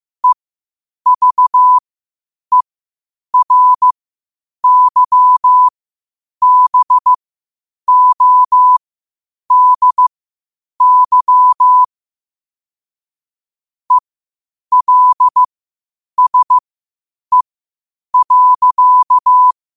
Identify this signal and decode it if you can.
Morse code, heard as a keyed tone, about 15 wpm, spaced as for 8 wpm, 1000 Hz. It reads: EVERYBODY ELSE.
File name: qotdfiles / 8wpm